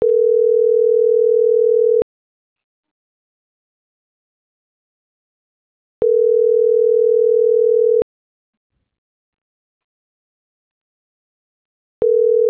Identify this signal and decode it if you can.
Ringback tone (North America), 440 + 480 Hz.